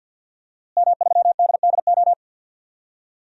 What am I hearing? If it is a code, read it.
Morse code, heard as a keyed tone, about 46 wpm, 700 Hz. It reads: M3DDO
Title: DXCC-100 Call Signs - 5 Characters - Repeated 3x - 50wpm